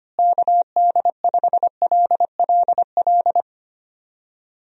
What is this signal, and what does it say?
Morse code, heard as a keyed tone, about 25 wpm, 700 Hz. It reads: KD5LLL
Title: USA Call Signs - 6 Characters - Repeated 3x - 25wpm